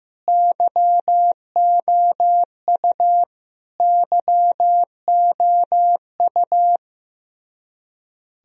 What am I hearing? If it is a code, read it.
Morse code, heard as a keyed tone, about 16 wpm, 700 Hz. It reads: YOU YOU
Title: Top 100 Words Repeated 15wpm